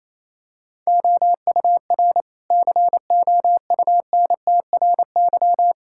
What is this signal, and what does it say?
Morse code, heard as a keyed tone, about 27 wpm, 700 Hz. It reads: OUR COUNTRY
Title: Sentences from Top 1000 Words - Review Part 5 - 28wpm